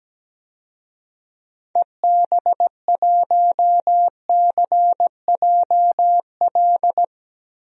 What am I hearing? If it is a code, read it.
Morse code, heard as a keyed tone, about 17 wpm, 700 Hz. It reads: EB1CJL
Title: DXCC-100 Call Signs - 6 Characters - 17wpm